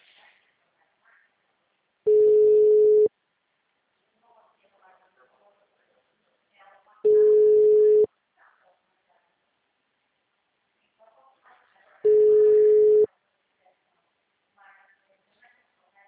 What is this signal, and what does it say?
Ringback tone (Europe, 425 Hz), with speech.